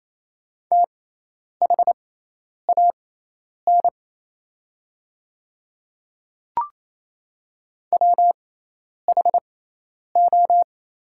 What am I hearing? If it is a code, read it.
Morse code, heard as a keyed tone, about 28 wpm, spaced as for 10 wpm, 700 Hz. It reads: THAN WHO
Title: 6x Character Spacing - Top 100 Words - 28wpm